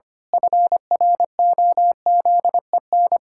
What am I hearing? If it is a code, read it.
Morse code, heard as a keyed tone, about 25 wpm, 700 Hz. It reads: FROZEN
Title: Pangram Challenge v2 - 25wpm